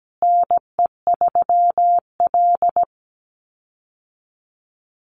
Morse code, heard as a keyed tone, about 17 wpm, 700 Hz. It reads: NE3L
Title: USA Call Signs Repeated 3x 17wpm